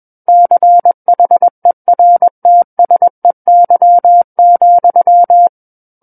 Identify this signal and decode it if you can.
Morse code, heard as a keyed tone, about 21 wpm, 700 Hz. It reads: CHERTSEY,
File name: Book One / 12-Chapter 12 - 21 WPM